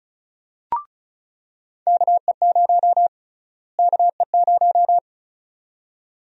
Morse code, heard as a keyed tone, about 34 wpm, 700 Hz. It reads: KE0 KE0
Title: USA Call Sign Prefixes - Repeated 2x - 35wpm